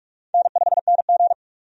Morse code, heard as a keyed tone, about 45 wpm, 700 Hz. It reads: N5NG
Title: USA Call Signs - 4 Characters - 45wpm